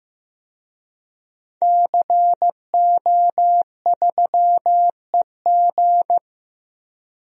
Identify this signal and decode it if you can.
Morse code, heard as a keyed tone, about 16 wpm, 700 Hz. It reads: CO3EG
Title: DXCC-100 Call Signs - 5 Characters - Repeated 2x - 15wpm